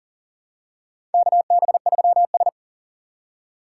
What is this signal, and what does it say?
Morse code, heard as a keyed tone, about 40 wpm, 700 Hz. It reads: KB3S
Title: USA Call Signs - 4 Characters - 3x Character Spacing - 40wpm